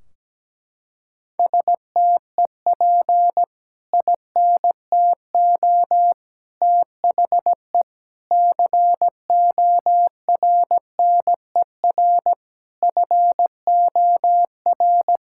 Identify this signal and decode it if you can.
Morse code, heard as a keyed tone, about 17 wpm, 700 Hz. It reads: STEP INTO THE CORNER FOR
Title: Sentences from Top 1600 Words 17wpm